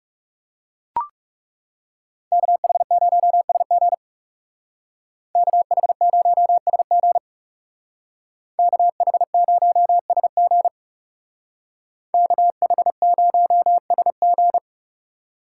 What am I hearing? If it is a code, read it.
Morse code, heard as a keyed tone, about 37 wpm, 700 Hz. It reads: KH0SG KH0SG KH0SG KH0SG
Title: Speed Racing - USA 2x2 Call Signs 30wpm